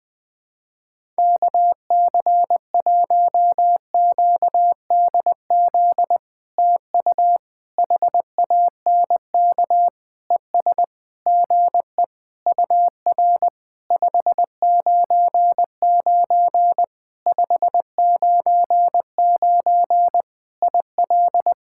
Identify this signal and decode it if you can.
Morse code, heard as a keyed tone, about 20 wpm, 700 Hz. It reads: KC1QDZ TU HANK ES GE UR 599 599 IL
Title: POTA QSO Practice - 20wpm